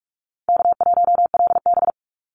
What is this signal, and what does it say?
Morse code, heard as a keyed tone, about 44 wpm, 700 Hz. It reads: K1LB
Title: NA SOTA Activator Callsigns 45wpm